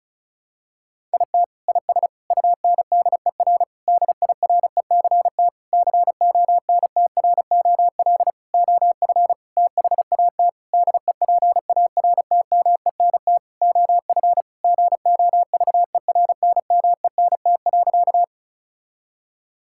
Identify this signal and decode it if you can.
Morse code, heard as a keyed tone, about 34 wpm, 700 Hz. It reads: IT IS UNDER DIRECT CONTROL OF THAT DEPARTMENT OF GOVERNMENT.
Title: Sentences from Top 1000 Words - Review Part 5 - 35wpm